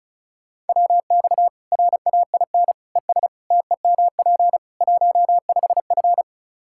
Morse code, heard as a keyed tone, about 36 wpm, 700 Hz. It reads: WX RAIN ES TEMP 15F
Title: Practice QSO - Exchange 2 - Info - 35wpm